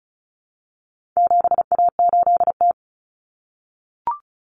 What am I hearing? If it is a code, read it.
Morse code, heard as a keyed tone, about 34 wpm, 700 Hz. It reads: 7A8T
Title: Common Cut Numbers - 4-Digit 35wpm